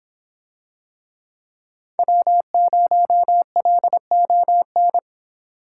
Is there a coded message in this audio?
Yes: Morse code from 2.0 to 5.0 s, heard as a keyed tone, about 24 wpm, 700 Hz. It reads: W0LON